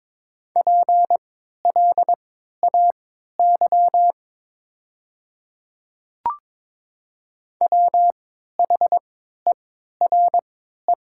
Morse code, heard as a keyed tone, about 22 wpm, spaced as for 13 wpm, 700 Hz. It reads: PLAY WHERE
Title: 3x Character Spacing - Top 200 Words - 22wpm